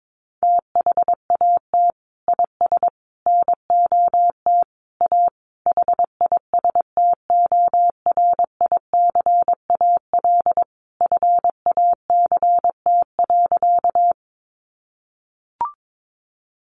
Morse code, heard as a keyed tone, about 22 wpm, 700 Hz. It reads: THAT IS NOT A HISTORICAL FACT.
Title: Sentences from Top 600 Words 22wpm